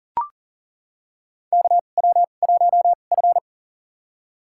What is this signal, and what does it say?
Morse code, heard as a keyed tone, about 42 wpm, spaced as for 29 wpm, 690 Hz. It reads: KW1F
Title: USA Call Signs - 4 Characters - 2x Character Spacing - 40wpm